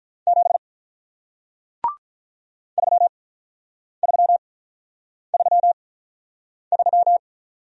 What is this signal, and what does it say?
Morse code, heard as a keyed tone, about 48 wpm, spaced as for 9 wpm, 700 Hz. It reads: 7 3333